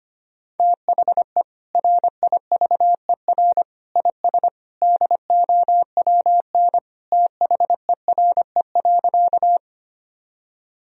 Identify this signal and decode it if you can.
Morse code, heard as a keyed tone, about 25 wpm, 700 Hz. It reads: THE RIVER IS DOWN THERE.
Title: Sentences from Top 1000 Words - Review Part 1 - 25wpm